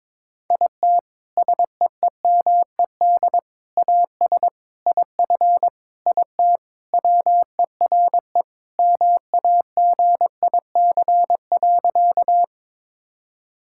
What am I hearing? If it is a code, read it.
Morse code, heard as a keyed tone, about 22 wpm, 700 Hz. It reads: IT SEEMED AS IF IT WERE MAGIC.
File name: practice / 22wpm / Sentences with 4-Word Phrases - 22wpm